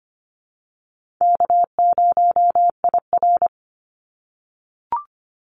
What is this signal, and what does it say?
Morse code, heard as a keyed tone, about 25 wpm, 700 Hz. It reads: K0IR
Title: Call Signs - 25wpm